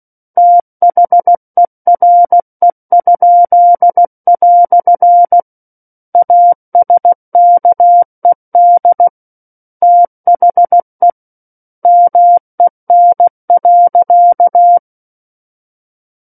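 Morse code, heard as a keyed tone, about 16 wpm, 700 Hz. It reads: THERE?" ASKED THE MEN.
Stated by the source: Morse practice of book text at 16 wpm.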